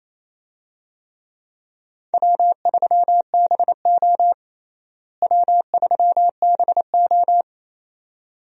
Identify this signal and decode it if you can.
Morse code, heard as a keyed tone, about 28 wpm, 700 Hz. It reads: W3BO W3BO
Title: USA Call Signs - 4 Characters - Repeated 2x - 28wpm